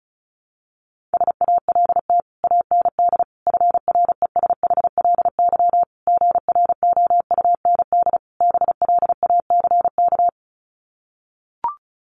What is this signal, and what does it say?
Morse code, heard as a keyed tone, about 34 wpm, 700 Hz. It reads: SALT AND FRESHLY GROUND BLACK
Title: Sets of 5 Words Encore 35wpm